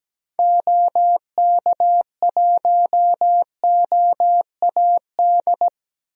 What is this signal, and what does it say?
Morse code, heard as a keyed tone, about 17 wpm, 700 Hz. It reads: OK1OAD
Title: DXCC-100 Call Signs - 6 Characters - 17wpm